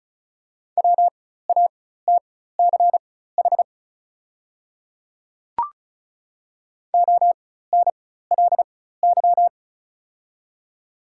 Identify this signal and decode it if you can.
Morse code, heard as a keyed tone, about 36 wpm, spaced as for 16 wpm, 700 Hz. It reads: WATCH ONLY
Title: 4x Character Spacing - Top 300 Words - 35wpm